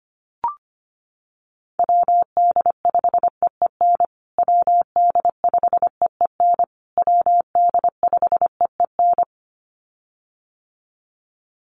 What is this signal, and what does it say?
Morse code, heard as a keyed tone, about 25 wpm, 700 Hz. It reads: WD5EEN WD5EEN WD5EEN
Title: USA Call Signs Repeated 3x 25wpm